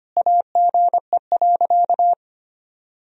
Morse code, heard as a keyed tone, about 25 wpm, 700 Hz. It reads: AGE.
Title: Sentences from Top 1000 Words - Review Part 3 - 25wpm